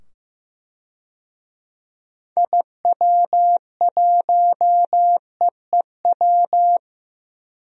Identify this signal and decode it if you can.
Morse code, heard as a keyed tone, about 15 wpm, 700 Hz. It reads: IW1EEW
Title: DXCC-100 Call Signs - 6 Characters - 15wpm